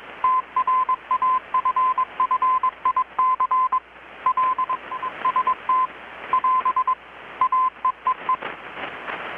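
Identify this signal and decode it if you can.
Morse code, heard as a keyed tone, about 21 wpm, 1030 Hz. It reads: TRAFFIC LST <AS> AEEE
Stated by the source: Morse from KPH's Night of Nights XX maritime radiotelegraphy event, received in Chiapas, Mexico about 3,800 km away on a Kenwood R-600 with a grounded 10 m long-wire antenna.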